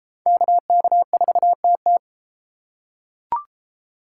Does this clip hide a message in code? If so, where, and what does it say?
Yes: Morse code from 0.3 to 2.0 s, heard as a keyed tone, about 33 wpm, 700 Hz. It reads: KK4TT